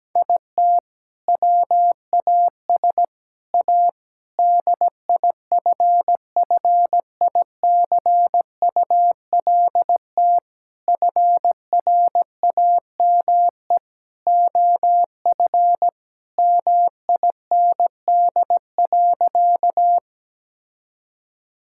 Morse code, heard as a keyed tone, about 17 wpm, 700 Hz. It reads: IT WAS A DIFFICULT FRAME OF MIND.
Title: Sentences from Top 1000 Words - Review Part 6 - 17wpm